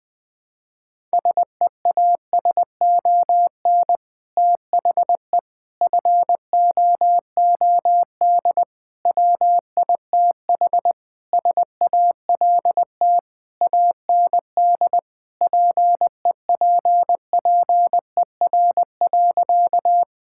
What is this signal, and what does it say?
Morse code, heard as a keyed tone, about 20 wpm, 700 Hz. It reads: SEASON THE FOOD WITH SALT AND PEPPER.
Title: Sentences with 5-Word Phrases - 20wpm